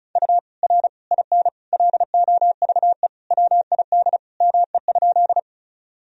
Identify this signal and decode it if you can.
Morse code, heard as a keyed tone, about 34 wpm, 700 Hz. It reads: U R IN LOVE WID ME?
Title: Sentences from Top 400 Words and Abbreviations 35wpm